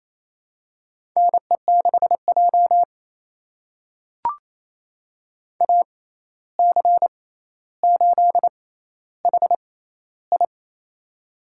Morse code, heard as a keyed tone, about 28 wpm, spaced as for 13 wpm, 700 Hz. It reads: NE6J AC8HI
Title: USA Call Signs - 6x Character Spacing - 28wpm